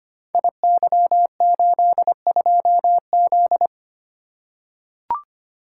Morse code, heard as a keyed tone, about 25 wpm, 700 Hz. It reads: IY82Z
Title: Code Groups - 5 Letter:Numbers - 25wpm